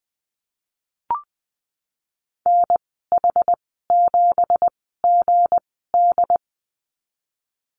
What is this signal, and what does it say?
Morse code, heard as a keyed tone, about 20 wpm, spaced as for 15 wpm, 700 Hz. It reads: NH7GD
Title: 179 Lesson 29g - Slash - Call Signs 20wpm